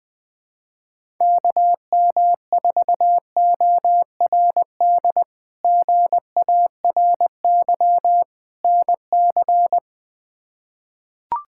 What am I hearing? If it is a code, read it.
Morse code, heard as a keyed tone, about 21 wpm, 700 Hz. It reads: KM4ORD GARY NC
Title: K1USN SST QSO Practice - 20wpm